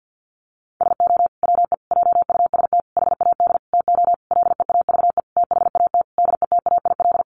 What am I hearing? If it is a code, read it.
Morse code, heard as a keyed tone, about 50 wpm, 700 Hz. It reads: SO WE JUST HAD TO LEAVE THAT DETAIL
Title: Sentences from Top 1500 Words 50wpm